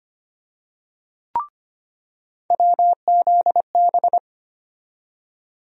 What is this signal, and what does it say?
Morse code, heard as a keyed tone, about 25 wpm, 700 Hz. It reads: WZB
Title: USA Call Sign Postfixes - 3 Letters 25wpm